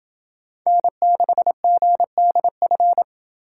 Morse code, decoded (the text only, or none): N6GDF